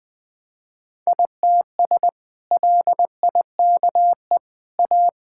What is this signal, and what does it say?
Morse code, heard as a keyed tone, about 20 wpm, 700 Hz. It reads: ITS LIKE A
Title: Sentences with 2-Word Phrases - 20wpm